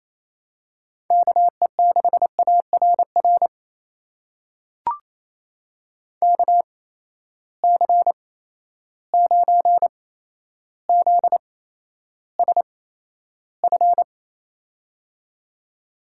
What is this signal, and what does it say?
Morse code, heard as a keyed tone, about 28 wpm, spaced as for 12 wpm, 700 Hz. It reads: KE6ARR KC9ZSF
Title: USA Call Signs - 6 Characters - 8x Character Spacing - 28wpm